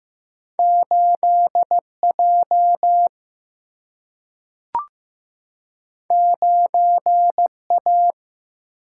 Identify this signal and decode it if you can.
Morse code, heard as a keyed tone, about 15 wpm, 700 Hz. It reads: 8 J 9 A
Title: Single-Letter Single-Number 15wpm